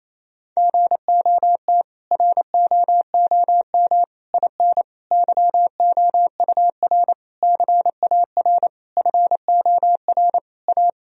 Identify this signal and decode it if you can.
Morse code, heard as a keyed tone, about 28 wpm, 700 Hz. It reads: GOT ROOM IN YOUR CAR FOR A
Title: Sentences from Top 1000 Words - Review Part 1 - 28wpm